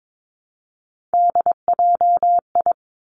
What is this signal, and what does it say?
Morse code, heard as a keyed tone, about 22 wpm, 700 Hz. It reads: DJI
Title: Three Challenging Letter-Number Combinations 22wpm